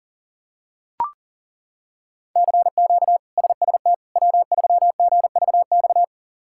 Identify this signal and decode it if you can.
Morse code, heard as a keyed tone, about 40 wpm, 700 Hz. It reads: CQ SST W3GVX
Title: K1USN SST QSO Practice - 40wpm